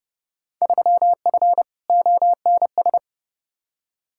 Morse code, heard as a keyed tone, about 32 wpm, 700 Hz. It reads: 3F ONS
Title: Field Day Exchanges - 30wpm